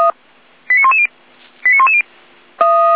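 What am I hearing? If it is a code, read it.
Five-tone selective call (ZVEI), heard as quick runs of five tones, tones of about 70 ms. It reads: 89110 89110
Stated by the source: ZVEI five-tone call received by SDR.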